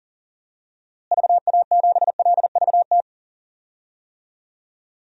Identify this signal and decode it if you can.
Morse code, heard as a keyed tone, about 40 wpm, 700 Hz. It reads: VA7LVT
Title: DXCC-100 Call Signs - 6 Characters - Repeated 3x - 40wpm